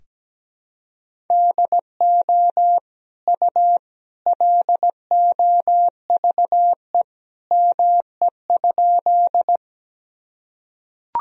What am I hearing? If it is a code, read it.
Morse code, heard as a keyed tone, about 17 wpm, 700 Hz. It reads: DO U LOVE ME?
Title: Sentences from Top 400 Words and Abbreviations 17wpm